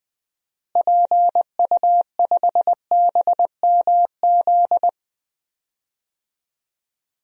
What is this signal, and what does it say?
Morse code, heard as a keyed tone, about 20 wpm, 700 Hz. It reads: PU5BMZ
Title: DXCC-100 Call Signs - 6 Characters - Repeated 3x - 20wpm